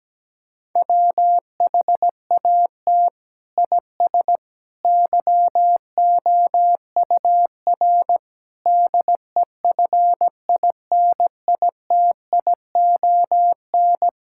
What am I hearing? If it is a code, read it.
Morse code, heard as a keyed tone, about 17 wpm, 700 Hz. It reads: WHAT IS YOUR DEFINITION
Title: Sentences from Top 1000 Words - Review Part 6 - 17wpm